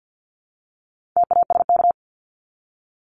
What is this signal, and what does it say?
Morse code, heard as a keyed tone, about 50 wpm, 700 Hz. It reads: TASK
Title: Top 751-1000 Words 50wpm